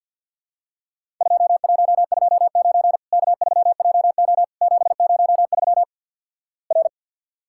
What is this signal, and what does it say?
Morse code, heard as a keyed tone, about 50 wpm, 700 Hz. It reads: 2129 K3JO 703 R
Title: Worked All Europe DX Contest - QTC Exchange - 50wpm